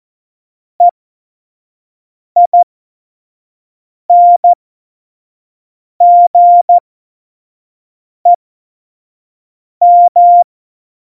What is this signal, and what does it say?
Morse code, heard as a keyed tone, about 14 wpm, spaced as for 5 wpm, 700 Hz. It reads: EINGEM
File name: Morsekurs / Witze / Witz 24 April 2018, 25 F70